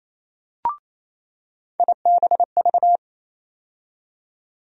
Morse code, heard as a keyed tone, about 28 wpm, 700 Hz. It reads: IBV
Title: USA Call Sign Postfixes - 3 Letters 28wpm